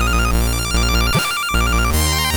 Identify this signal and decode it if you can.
There is no code.